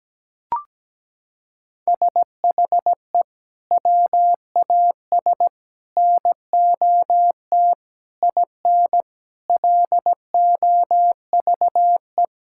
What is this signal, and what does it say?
Morse code, heard as a keyed tone, about 17 wpm, 700 Hz. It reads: SHE WAS NOT IN LOVE
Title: Sentences from Top 1000 Words - Review Part 5 - 17wpm